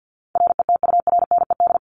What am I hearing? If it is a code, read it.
Morse code, heard as a keyed tone, about 48 wpm, 700 Hz. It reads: RETURNED
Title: ED Suffix - Top 100 Words with Base Words 50wpm